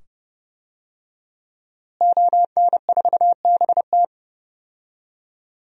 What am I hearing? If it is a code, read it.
Morse code, heard as a keyed tone, about 30 wpm, 700 Hz. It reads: ON4BT
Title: DXCC-100 Call Signs - 5 Characters - Repeated 2x - 30wpm